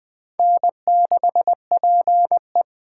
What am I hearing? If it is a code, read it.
Morse code, heard as a keyed tone, about 24 wpm, 700 Hz. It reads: N6PE
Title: Speed Racing - USA 1x2 Call Signs 20wpm